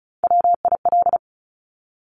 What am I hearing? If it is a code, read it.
Morse code, heard as a keyed tone, about 36 wpm, 700 Hz. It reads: WIL
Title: Top Three Letter Combinations in Words 35wpm